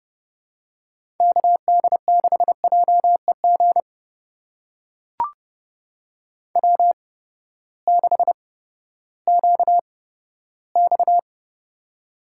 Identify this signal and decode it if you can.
Morse code, heard as a keyed tone, about 32 wpm, spaced as for 15 wpm, 700 Hz. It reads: KD6JEG W6QX
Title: USA Call Signs - 8x Character Spacing - 30wpm